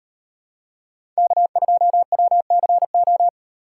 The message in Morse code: K2WCO

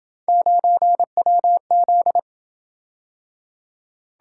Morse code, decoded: 9WZ

